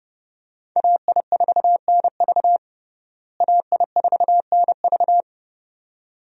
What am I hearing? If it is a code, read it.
Morse code, heard as a keyed tone, about 30 wpm, 700 Hz. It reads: AI4NV AI4NV
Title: USA Call Signs - 5 Characters - Repeated 2x - 30wpm